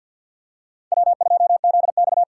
Morse code, heard as a keyed tone, about 50 wpm, 700 Hz. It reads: W2ZX
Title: USA Call Signs - 4 Characters - 6x Character Spacing - 50wpm